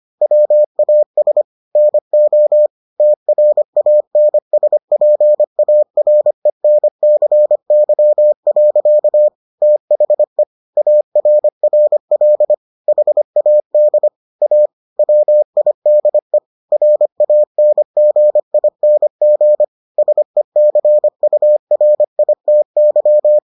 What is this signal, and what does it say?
Morse code, heard as a keyed tone, about 25 wpm, 600 Hz. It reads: WAS NO TRANSPARENCY. THE ARRL HAD A WIDE RANGING SECURITY